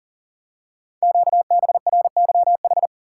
Morse code, heard as a keyed tone, about 44 wpm, 700 Hz. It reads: QBRYH
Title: Code Groups - 5 Letters - 40wpm